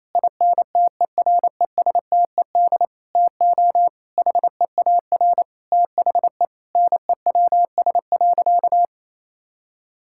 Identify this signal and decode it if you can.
Morse code, heard as a keyed tone, about 28 wpm, 700 Hz. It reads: INTERESTED TO HEAR THE NEWS.
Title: Sentences from Top 1000 Words - Review Part 6 - 28wpm